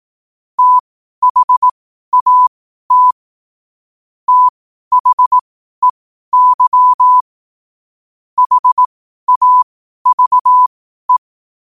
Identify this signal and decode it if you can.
Morse code, heard as a keyed tone, about 18 wpm, spaced as for 12 wpm, 1000 Hz. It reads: THAT THEY HAVE
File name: qotdfiles / 12wpm